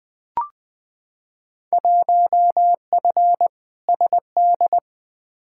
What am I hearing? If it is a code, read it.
Morse code, heard as a keyed tone, about 20 wpm, 700 Hz. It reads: 1F SD
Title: Field Day Exchanges - 20wpm